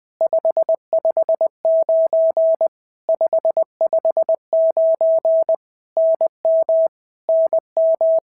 Morse code, heard as a keyed tone, about 20 wpm, 650 Hz. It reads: 559 559 NM NM
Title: POTA QSO Practice - 20wpm